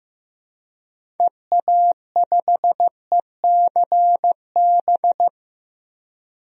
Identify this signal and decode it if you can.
Morse code, heard as a keyed tone, about 15 wpm, 700 Hz. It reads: EA5ECB
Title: DXCC-100 Call Signs - 6 Characters - 15wpm